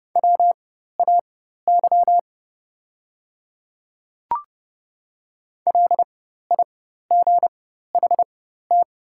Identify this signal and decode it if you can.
Morse code, heard as a keyed tone, about 30 wpm, spaced as for 14 wpm, 700 Hz. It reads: WAY LIGHT
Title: 4x Character Spacing - Top 200 Words - 30wpm